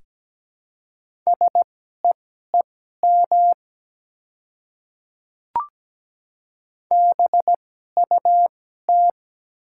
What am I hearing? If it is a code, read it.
Morse code, heard as a keyed tone, about 17 wpm, spaced as for 12 wpm, 690 Hz. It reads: SEEM BUT
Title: 2x Character Spacing - Top 300 Words - 17wpm